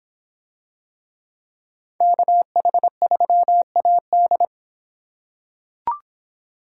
Morse code, heard as a keyed tone, about 26 wpm, 700 Hz. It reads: KH3AD